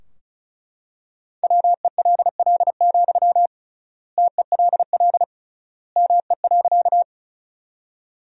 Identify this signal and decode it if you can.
Morse code, heard as a keyed tone, about 34 wpm, 700 Hz. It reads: WELL, TELL ME.
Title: 3x Word-Spacing - Sentences from Top 200 Words - 35wpm